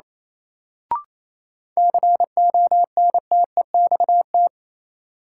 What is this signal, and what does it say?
Morse code, heard as a keyed tone, about 27 wpm, 700 Hz. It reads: CONTEXT
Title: Sentences from Top 1000 Words - Review Part 7 - New Word Drill - 28wpm